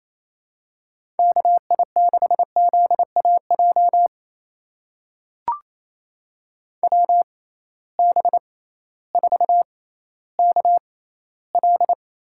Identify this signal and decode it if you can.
Morse code, heard as a keyed tone, about 28 wpm, spaced as for 15 wpm, 700 Hz. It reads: KI6ZAJ WB4KL
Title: USA Call Signs - 6 Characters - 6x Character Spacing - 28wpm